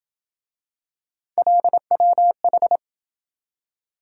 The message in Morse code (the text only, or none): LWH